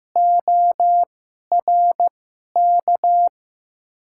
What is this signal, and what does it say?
Morse code, heard as a keyed tone, about 15 wpm, spaced as for 11 wpm, 700 Hz. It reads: ORK
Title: 2x Character Spacing - Top 500 Words - 15wpm